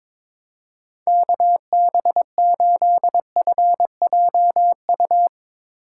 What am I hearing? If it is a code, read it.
Morse code, heard as a keyed tone, about 22 wpm, 700 Hz. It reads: KB8FJU
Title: Speed Racing - USA 2x3 Call Signs 22wpm